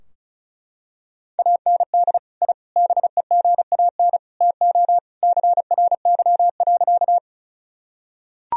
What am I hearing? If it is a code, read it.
Morse code, heard as a keyed tone, about 34 wpm, 700 Hz. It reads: AND I BEGAN TO CRY.
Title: Sentences from Top 1000 Words - Review Part 2 - 35wpm